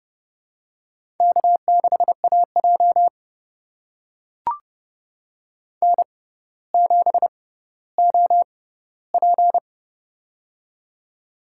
Morse code, heard as a keyed tone, about 31 wpm, spaced as for 16 wpm, 700 Hz. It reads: K6AJ N7OP
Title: USA Call Signs - 4 Characters - 6x Character Spacing - 30wpm